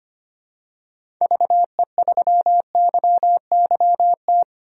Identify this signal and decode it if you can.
Morse code, heard as a keyed tone, about 25 wpm, 700 Hz. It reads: VE3YYT
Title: DXCC-100 Call Signs - 6 Characters - 25wpm